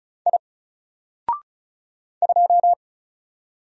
Morse code, heard as a keyed tone, about 36 wpm, 700 Hz. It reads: I 2
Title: Single Letter-Number Alt 35wpm